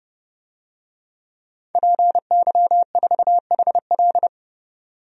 Morse code, heard as a keyed tone, about 29 wpm, 700 Hz. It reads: PY4HL
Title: DXCC-100 Call Signs - 5 Characters - 30wpm